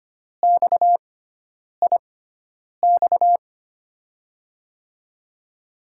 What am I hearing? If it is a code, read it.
Morse code, heard as a keyed tone, about 25 wpm, spaced as for 9 wpm, 700 Hz. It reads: XIX
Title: USA Call Signs - 5 Characters - 6x Character Spacing - 25wpm